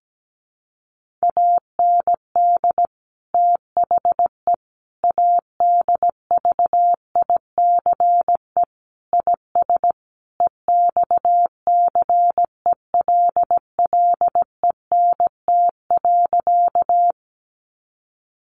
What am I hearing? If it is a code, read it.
Morse code, heard as a keyed tone, about 17 wpm, 700 Hz. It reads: AND THE ADVICE IS EXCELLENT.
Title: Sentences from Top 1100 Words 17wpm